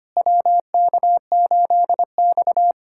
Morse code, heard as a keyed tone, about 25 wpm, 700 Hz. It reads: WK8X
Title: USA Callsign DE Callsign - 4 Characters - 25wpm